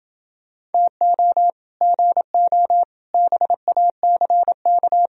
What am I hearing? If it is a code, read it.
Morse code, heard as a keyed tone, about 29 wpm, 700 Hz. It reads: TO GO BACK